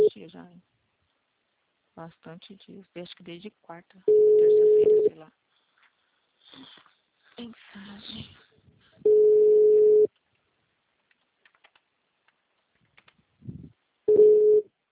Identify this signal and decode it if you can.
Ringback tone (Europe, 425 Hz), with speech.